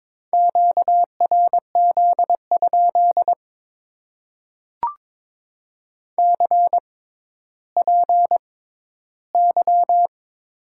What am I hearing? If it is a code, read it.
Morse code, heard as a keyed tone, about 22 wpm, spaced as for 13 wpm, 700 Hz. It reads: QRZ? CPY
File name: Top 90 QSO Elements - 6x Character Spacing - 22wpm